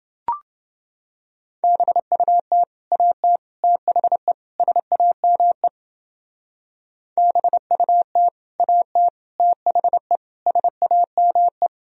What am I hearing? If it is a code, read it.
Morse code, heard as a keyed tone, about 29 wpm, 700 Hz. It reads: BUT AT THE SAME BUT AT THE SAME